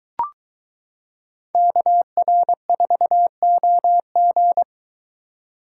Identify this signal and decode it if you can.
Morse code, heard as a keyed tone, about 22 wpm, 700 Hz. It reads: KR4OG